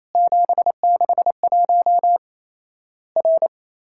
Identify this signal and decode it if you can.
Morse code, heard as a keyed tone, about 27 wpm, 700 Hz. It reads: 761 R
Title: Worked All Europe DX Contest - QTC Exchange - 28wpm